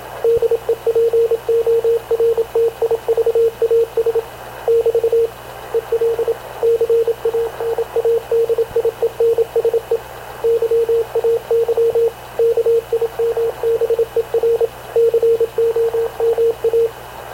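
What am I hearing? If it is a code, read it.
Morse code, heard as a keyed tone, about 26 wpm, 460 Hz, with radio noise. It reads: DEPORTIVAS <BT> EL CANADIENSE YAY KIMBER COMA
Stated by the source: Havana Radio / CLA sending Spanish press (PX) in Morse, recorded by Radio Officer Paul Dumesnil onboard CNAV QUEST/CZDO.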